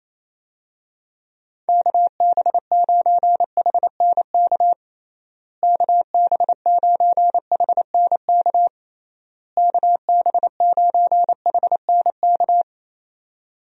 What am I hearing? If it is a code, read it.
Morse code, heard as a keyed tone, about 28 wpm, 700 Hz. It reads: KB9HNK KB9HNK KB9HNK
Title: USA Call Signs - 6 Characters - Repeated 3x - 28wpm